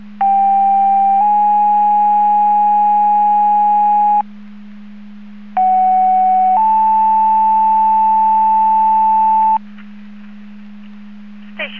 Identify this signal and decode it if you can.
Two-tone page, heard as a one-second tone, then a three-second tone, about 790 then 830 Hz, about 750 then 880 Hz. It is heard from 0.2 to 9.6 s.
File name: Portage 154.385 Paging